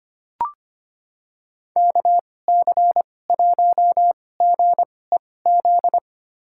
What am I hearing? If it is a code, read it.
Morse code, heard as a keyed tone, about 25 wpm, spaced as for 18 wpm, 700 Hz. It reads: KC1GEZ
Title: USA Call Signs - 6 Characters - 2x Character Spacing - 25wpm